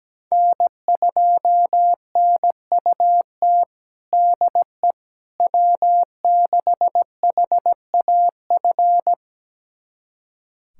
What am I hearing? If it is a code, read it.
Morse code, heard as a keyed tone, about 17 wpm, 700 Hz. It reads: N2NUT DE W6HAF
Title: USA Callsign DE Callsign - 5 Characters - 17wpm